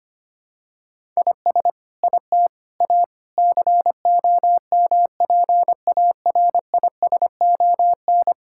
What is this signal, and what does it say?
Morse code, heard as a keyed tone, about 25 wpm, 700 Hz. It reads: IS IT A COMPARISON
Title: Sentences from Top 1600 Words 25wpm